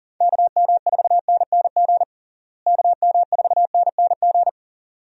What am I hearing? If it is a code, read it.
Morse code, heard as a keyed tone, about 40 wpm, 700 Hz. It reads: KM4NNG KM4NNG
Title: USA Call Signs - 6 Characters - Repeated 2x - 40wpm